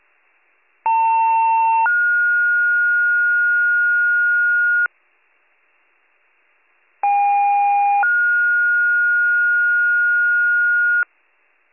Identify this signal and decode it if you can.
Two-tone page, heard as a one-second tone, then a three-second tone, about 900 then 1430 Hz, about 810 then 1430 Hz. It is heard from 0.8 to 11.0 s.